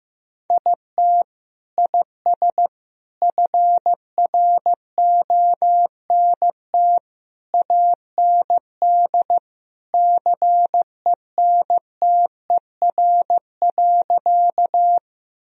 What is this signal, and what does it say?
Morse code, heard as a keyed tone, about 15 wpm, 700 Hz. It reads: IT IS FRONT AND CENTER.
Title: Sentences from Top 1000 Words - Review Part 2 - 15wpm